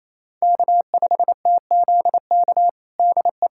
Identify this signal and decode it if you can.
Morse code, heard as a keyed tone, about 28 wpm, 700 Hz. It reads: K5TZK DE
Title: USA Callsign DE Callsign - 5 Characters - 28wpm